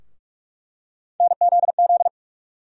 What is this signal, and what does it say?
Morse code, heard as a keyed tone, about 46 wpm, 700 Hz. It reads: NZZ
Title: USA Call Sign Postfixes - 3 Letters 45wpm